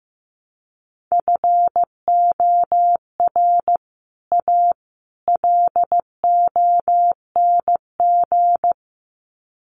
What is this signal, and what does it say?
Morse code, heard as a keyed tone, about 15 wpm, 700 Hz. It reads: FOR A LONG
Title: Sets of 3 Words 15wpm